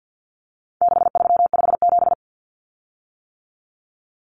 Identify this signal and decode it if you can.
Morse code, heard as a keyed tone, about 49 wpm, 700 Hz. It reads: 6357
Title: Common Cut Numbers - 4-Digit 50wpm